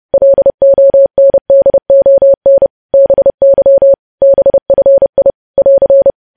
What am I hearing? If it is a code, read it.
Morse code, heard as a keyed tone, about 30 wpm, 550 Hz. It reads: LONDON BY BFI <AR>
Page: news headlines in morse code at 30 wpm